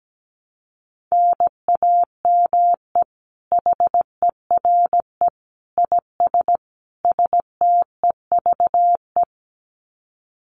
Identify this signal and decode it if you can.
Morse code, heard as a keyed tone, about 17 wpm, 700 Hz. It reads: NAME HERE IS STEVE
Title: Common QSO Phrases 17wpm